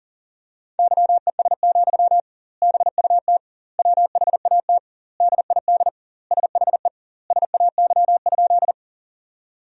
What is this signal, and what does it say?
Morse code, heard as a keyed tone, about 40 wpm, 700 Hz. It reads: YES, BUT WHAT DID SHE SAY?
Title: 2x Word-Spacing - Sentences from Top 500 Words - 40wpm